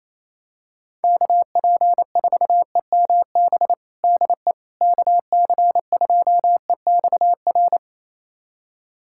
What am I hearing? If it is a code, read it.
Morse code, heard as a keyed tone, about 28 wpm, 700 Hz. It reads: KP4EMB DE KC2EXR
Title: USA Callsign DE Callsign 28wpm